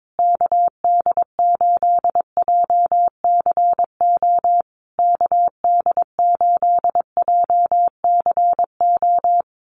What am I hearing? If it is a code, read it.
Morse code, heard as a keyed tone, about 22 wpm, 700 Hz. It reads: KD8JCO KD8JCO
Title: USA Call Signs Repeated 3x 22wpm